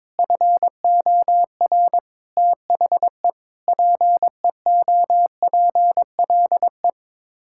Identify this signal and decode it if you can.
Morse code, heard as a keyed tone, about 22 wpm, 700 Hz. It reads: FOR THE PEOPLE
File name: Sets of 9 Words - 22wpm